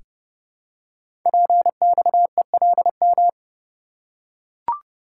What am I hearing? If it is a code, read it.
Morse code, heard as a keyed tone, about 30 wpm, 700 Hz. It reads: PXELM
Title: Code Groups - 5 Letters - 30wpm